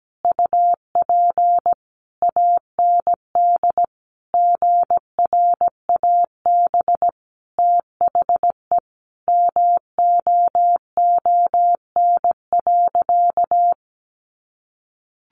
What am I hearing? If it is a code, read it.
Morse code, heard as a keyed tone, about 17 wpm, 700 Hz. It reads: UP AND GRAB THE MOON.
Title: A Sound Of Thunder 17wpm